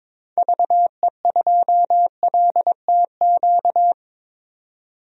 Morse code, heard as a keyed tone, about 22 wpm, 700 Hz. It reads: VE2LTQ
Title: DXCC-100 Call Signs - 6 Characters - 22wpm